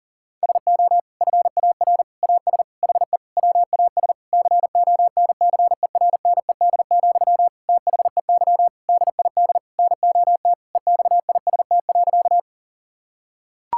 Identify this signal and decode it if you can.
Morse code, heard as a keyed tone, about 40 wpm, 700 Hz. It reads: SO FAR AS HE WAS CONCERNED, THEY DID NOT EXIST.
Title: Sentences from Top 1000 Words - Review Part 6 - 40wpm